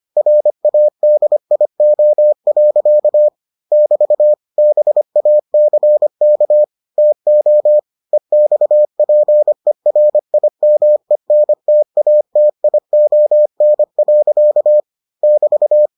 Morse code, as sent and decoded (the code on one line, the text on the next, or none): .-. .- -.. .. --- .-.-.- / -...- / -... .- -.-. -.- / - --- / . -..- .--. .-. .. -- . -. - .- - .. --- -. .-.-.- / -...-
RADIO. <BT> BACK TO EXPERIMENTATION. <BT>